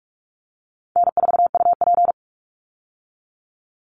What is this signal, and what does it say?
Morse code, heard as a keyed tone, about 44 wpm, 700 Hz. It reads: N4UP
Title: Easy Call Signs Repeated 2x - 45wpm